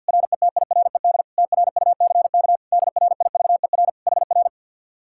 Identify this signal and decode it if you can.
Morse code, heard as a keyed tone, about 65 wpm, spaced as for 57 wpm, 700 Hz. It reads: RETIRED TRUCK DRIVER HR
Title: Speed-Racing Common QSO Phrases - 50wpm